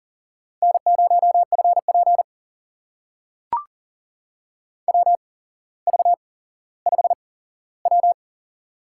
Morse code, heard as a keyed tone, about 38 wpm, spaced as for 17 wpm, 700 Hz. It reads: N0FP WV5W
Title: USA Call Signs - 4 Characters - 8x Character Spacing - 40wpm